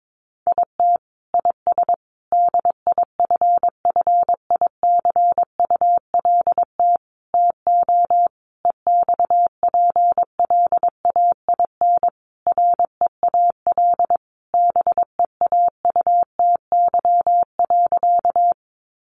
Morse code, heard as a keyed tone, about 22 wpm, 700 Hz. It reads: IT IS DIFFICULT TO EXPLAIN REAL BEAUTY.
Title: Sentences from Top 1100 Words 22wpm